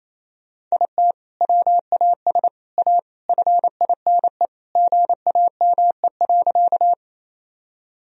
Morse code, heard as a keyed tone, about 28 wpm, 700 Hz. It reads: IT WAS A FINE GAME.
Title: Sentences from Top 1000 Words - Review Part 3 - 28wpm